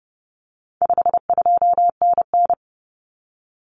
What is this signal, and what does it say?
Morse code, heard as a keyed tone, about 30 wpm, 700 Hz. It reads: 52NN
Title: Common Cut Numbers - 4-Digit 30wpm